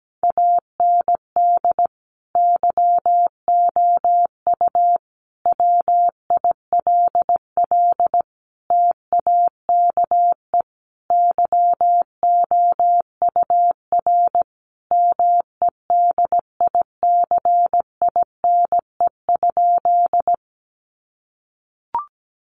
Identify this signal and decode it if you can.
Morse code, heard as a keyed tone, about 17 wpm, 700 Hz. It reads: AND YOU WILL TAKE YOUR MEDICINE?
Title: Sentences from Top 900 Words 17wpm